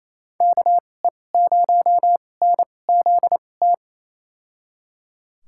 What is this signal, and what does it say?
Morse code, heard as a keyed tone, about 28 wpm, spaced as for 20 wpm, 700 Hz. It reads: KE0NZT
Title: USA Call Signs - 6 Characters - 2x Character Spacing - 28wpm